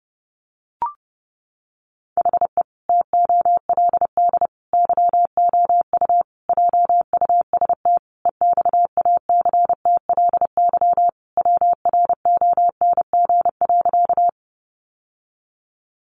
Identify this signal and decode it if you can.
Morse code, heard as a keyed tone, about 30 wpm, 700 Hz. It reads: HE TOLD YOU JUST EXACTLY WRONG.
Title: Sentences from Top 900 Words 30wpm